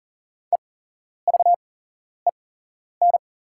Morse code, heard as a keyed tone, about 40 wpm, spaced as for 11 wpm, 690 Hz. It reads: EVEN
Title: 8x Character Spacing - Top 200 Words - 40wpm